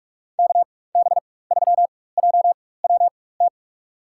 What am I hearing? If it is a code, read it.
Morse code, heard as a keyed tone, about 46 wpm, spaced as for 21 wpm, 700 Hz. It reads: KB3JWT
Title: USA Call Signs - 4x Character Spacing - 45wpm